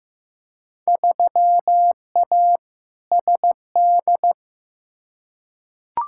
Morse code, heard as a keyed tone, about 15 wpm, 700 Hz. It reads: 3A SD
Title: Field Day Exchanges - 15wpm